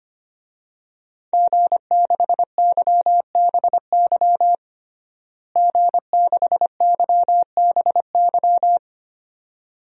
Morse code, heard as a keyed tone, about 25 wpm, 700 Hz. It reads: G6YBY G6YBY
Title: DXCC-100 Call Signs - 5 Characters - Repeated 2x - 25wpm